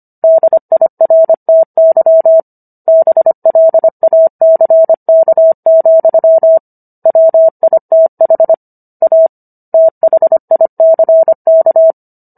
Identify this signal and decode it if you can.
Morse code, heard as a keyed tone, about 25 wpm, 650 Hz. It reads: DIRTY BLACK, WITH A THICK